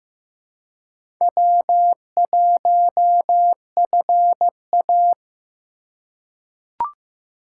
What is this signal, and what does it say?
Morse code, heard as a keyed tone, about 15 wpm, 700 Hz. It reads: W1FA
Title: USA Call Signs - 4 Characters - 4x Character Spacing - 15wpm